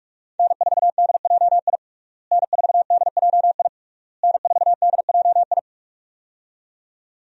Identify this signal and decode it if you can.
Morse code, heard as a keyed tone, about 41 wpm, 700 Hz. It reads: N4DJI N4DJI N4DJI
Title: USA Call Signs - 5 Characters - Repeated 3x - 45wpm